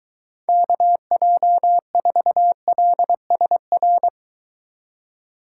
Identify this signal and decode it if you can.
Morse code, heard as a keyed tone, about 23 wpm, 700 Hz. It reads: KJ4LSR